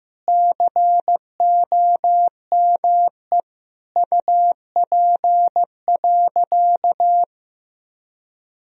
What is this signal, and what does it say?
Morse code, heard as a keyed tone, about 15 wpm, 700 Hz. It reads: COME UP.
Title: Sentences from Top 1000 Words - Review Part 2 - 15wpm